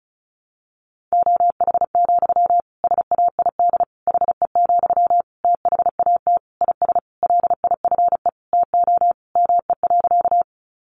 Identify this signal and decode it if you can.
Morse code, heard as a keyed tone, about 34 wpm, 700 Hz. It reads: OH, SAID HE, THAT IS LIFE TO ME.
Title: Sentences from Top 500 Words 35wpm